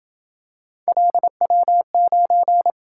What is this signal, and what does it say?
Morse code, heard as a keyed tone, about 26 wpm, 700 Hz. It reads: LW9